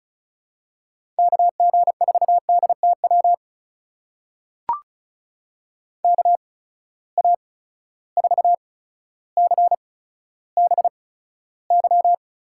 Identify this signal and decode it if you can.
Morse code, heard as a keyed tone, about 36 wpm, spaced as for 15 wpm, 700 Hz. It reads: KG4DTW KA4CBY
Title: USA Call Signs - 6 Characters - 8x Character Spacing - 35wpm